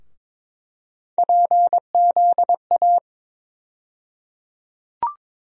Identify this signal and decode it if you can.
Morse code, heard as a keyed tone, about 22 wpm, 700 Hz. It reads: PZA
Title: USA Call Sign Postfixes - 3 Letters 22wpm